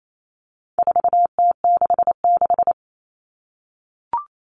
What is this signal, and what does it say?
Morse code, heard as a keyed tone, about 28 wpm, 700 Hz. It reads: 4T66
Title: All Cut Numbers - 4-Digit 28wpm